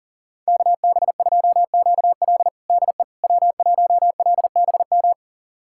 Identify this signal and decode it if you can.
Morse code, heard as a keyed tone, about 40 wpm, 700 Hz. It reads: KB2QL DE W1LBM
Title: USA Callsign DE Callsign - 5 Characters - 40wpm